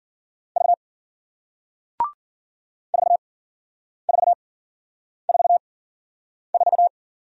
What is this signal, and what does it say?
Morse code, heard as a keyed tone, about 54 wpm, 700 Hz. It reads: V 4 4 4 4